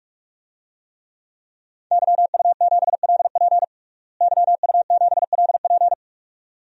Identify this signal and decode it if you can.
Morse code, heard as a keyed tone, about 46 wpm, 700 Hz. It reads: YU7LP YU7LP
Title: DXCC-100 Call Signs - 5 Characters - Repeated 2x - 45wpm